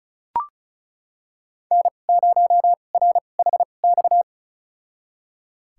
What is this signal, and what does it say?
Morse code, heard as a keyed tone, about 36 wpm, spaced as for 25 wpm, 700 Hz. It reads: N0RHX
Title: USA Call Signs - 5 Characters - 2x Character Spacing - 35wpm